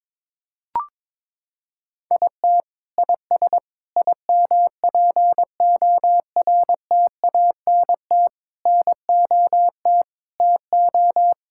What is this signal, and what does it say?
Morse code, heard as a keyed tone, about 22 wpm, 700 Hz. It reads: IT IS IMPORTANT NOT TO
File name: Sets of 9 Words - 22wpm